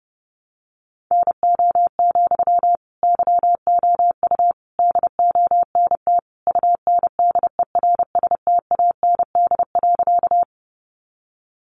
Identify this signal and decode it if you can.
Morse code, heard as a keyed tone, about 30 wpm, 700 Hz. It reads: NO, YOU DONT UNDERSTAND.
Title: Sentences from Top 600 Words 30wpm